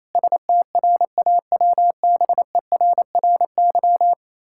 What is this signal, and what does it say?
Morse code, heard as a keyed tone, about 28 wpm, 700 Hz. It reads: STRAWBERRY
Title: Pangram Challenge v2 - 28wpm